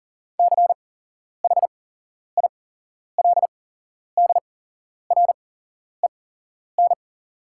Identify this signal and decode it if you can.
Morse code, heard as a keyed tone, about 40 wpm, spaced as for 11 wpm, 700 Hz. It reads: CHILDREN
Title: 8x Character Spacing - Top 300 Words - 40wpm